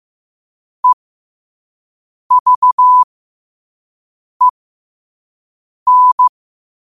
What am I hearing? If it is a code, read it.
Morse code, heard as a keyed tone, about 15 wpm, spaced as for 5 wpm, 1000 Hz. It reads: EVEN
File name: qotdfiles / 5wpm